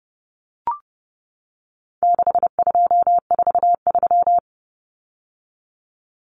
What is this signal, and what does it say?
Morse code, heard as a keyed tone, about 28 wpm, 700 Hz. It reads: 6243
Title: Numbers - 4-Digit 30wpm